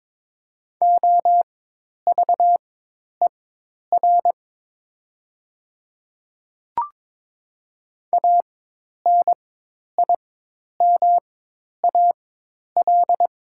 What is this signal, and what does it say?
Morse code, heard as a keyed tone, about 22 wpm, spaced as for 10 wpm, 700 Hz. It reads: OVER ANIMAL
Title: 4x Character Spacing - Top 200 Words - 22wpm